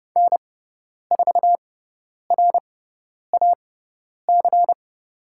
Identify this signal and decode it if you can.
Morse code, heard as a keyed tone, about 30 wpm, spaced as for 10 wpm, 700 Hz. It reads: N4RAC
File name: Lesson 17g - Letter u - Call Signs 30wpm